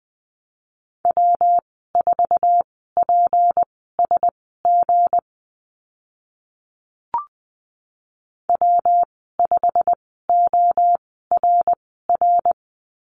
Morse code, heard as a keyed tone, about 20 wpm, spaced as for 15 wpm, 700 Hz. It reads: W4PSG W5ORR
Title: 161 Lesson 26g - Letter g - Call Signs 20wpm